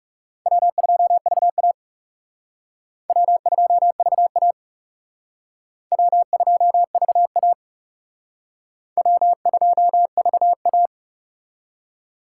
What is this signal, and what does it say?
Morse code, heard as a keyed tone, about 38 wpm, 700 Hz. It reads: W2VA W2VA W2VA W2VA